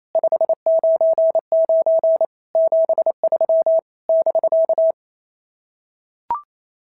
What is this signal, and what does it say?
Morse code, heard as a keyed tone, about 28 wpm, 690 Hz. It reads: 599 73 <BK>
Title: SOTA QSO Practice 28wpm